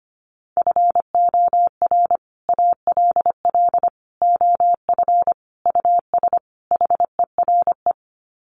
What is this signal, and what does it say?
Morse code, heard as a keyed tone, about 25 wpm, 700 Hz. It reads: FOR ALL OF US HERE
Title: Sets of 5 Words 25wpm